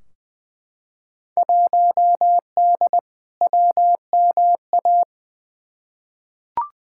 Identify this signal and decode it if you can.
Morse code, heard as a keyed tone, about 20 wpm, 700 Hz. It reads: 1D WMA
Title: Field Day Exchanges - 20wpm